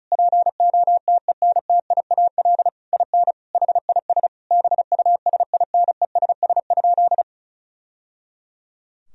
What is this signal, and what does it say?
Morse code, heard as a keyed tone, about 36 wpm, 700 Hz. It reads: POTENTIAL IN HIS BUSINESS?
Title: Sentences from Top 1000 Words - Review Part 4 - 35wpm